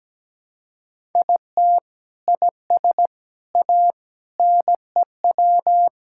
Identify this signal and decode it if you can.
Morse code, heard as a keyed tone, about 17 wpm, 700 Hz. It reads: IT IS A NEW
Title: Sentences from Top 1000 Words - Review Part 6 - 17wpm